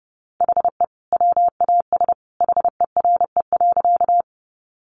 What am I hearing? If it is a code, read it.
Morse code, heard as a keyed tone, about 30 wpm, 700 Hz. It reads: HE WAS HERE.
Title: Sentences from Top 200 Words 30wpm